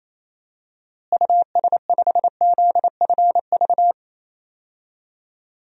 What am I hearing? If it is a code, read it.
Morse code, heard as a keyed tone, about 28 wpm, 700 Hz. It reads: US5ZFV
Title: DXCC-100 Call Signs - 6 Characters - Repeated 3x - 28wpm